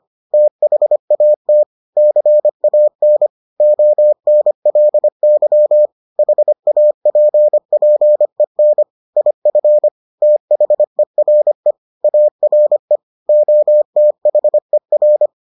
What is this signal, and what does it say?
Morse code, heard as a keyed tone, about 25 wpm, 600 Hz. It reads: THAT CAN ONLY HAPPEN IF THERE ARE OTHER